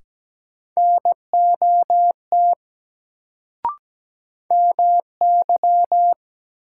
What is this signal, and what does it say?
Morse code, heard as a keyed tone, about 17 wpm, 700 Hz. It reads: NOT MY
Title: Sets of 2 Words - New Word Drill - 17wpm